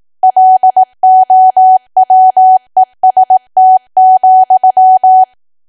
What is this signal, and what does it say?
Morse code, heard as a keyed tone, about 18 wpm, 750 Hz. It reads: LOWEST,
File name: cwa / W1AW-Level2 / 18WPM